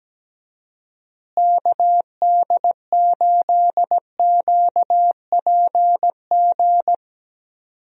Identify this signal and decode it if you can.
Morse code, heard as a keyed tone, about 17 wpm, 700 Hz. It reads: KD8QPG
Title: USA Call Signs - 6 Characters - Repeated 3x - 17wpm